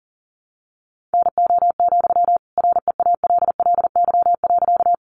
Morse code, heard as a keyed tone, about 40 wpm, 700 Hz. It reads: NO, REALLY.
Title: Sentences from Top 600 Words 40wpm